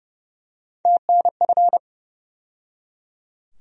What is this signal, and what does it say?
Morse code, heard as a keyed tone, about 30 wpm, 700 Hz. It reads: TNF
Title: USA Call Sign Postfixes - 3 Letters 30wpm